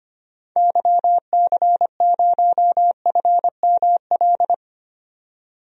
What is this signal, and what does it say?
Morse code, heard as a keyed tone, about 25 wpm, 700 Hz. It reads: YC0FML
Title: DXCC-100 Call Signs - 6 Characters - 25wpm